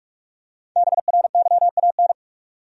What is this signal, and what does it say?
Morse code, heard as a keyed tone, about 42 wpm, 700 Hz. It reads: BRYAN
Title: USA Common Names - Max 5 Letters - 45wpm